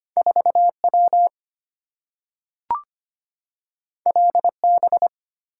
Morse code, heard as a keyed tone, about 25 wpm, 700 Hz. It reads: 4W LB
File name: Two Letter-Number Combinations - 25wpm